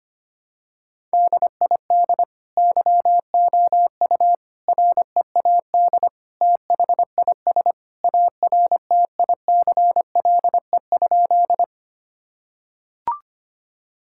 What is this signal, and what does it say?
Morse code, heard as a keyed tone, about 25 wpm, 700 Hz. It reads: DID YOU READ THIS ARTICLE?
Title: Sentences from Top 1000 Words - Review Part 5 - 25wpm